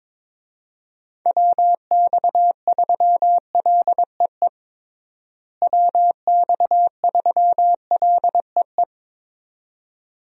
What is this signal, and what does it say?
Morse code, heard as a keyed tone, about 22 wpm, 700 Hz. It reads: WX3LEE WX3LEE
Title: USA Call Signs - 6 Characters - Repeated 2x - 22wpm